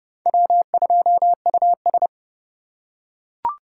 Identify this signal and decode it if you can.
Morse code, heard as a keyed tone, about 29 wpm, 700 Hz. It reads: W2US